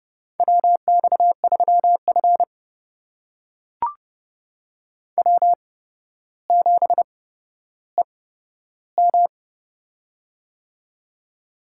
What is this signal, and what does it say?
Morse code, heard as a keyed tone, about 30 wpm, spaced as for 13 wpm, 700 Hz. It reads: WX3F W7EM
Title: USA Call Signs - 8x Character Spacing - 30wpm